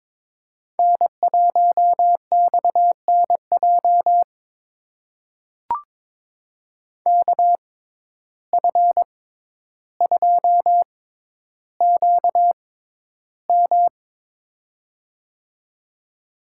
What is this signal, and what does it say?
Morse code, heard as a keyed tone, about 22 wpm, spaced as for 11 wpm, 700 Hz. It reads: N1XNJ KF2QM
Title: USA Call Signs - 5 Characters - 6x Character Spacing - 22wpm